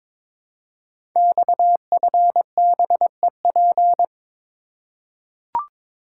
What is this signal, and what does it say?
Morse code, heard as a keyed tone, about 22 wpm, 700 Hz. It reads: XFBEP
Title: Code Groups - 5 Letter:Numbers - 22wpm